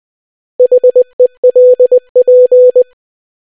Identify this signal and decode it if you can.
Morse code, heard as a keyed tone, about 20 wpm, 500 Hz. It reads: HELP